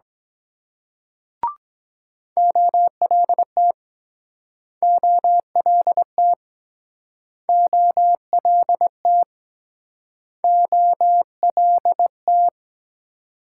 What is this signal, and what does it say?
Morse code, heard as a keyed tone, about 24 wpm, 700 Hz. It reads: OLT OLT OLT OLT